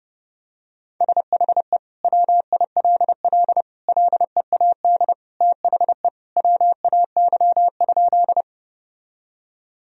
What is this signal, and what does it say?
Morse code, heard as a keyed tone, about 30 wpm, 700 Hz. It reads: SHE WILL LEAD THE WAY?
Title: Sentences from Top 1000 Words - Review Part 2 - 30wpm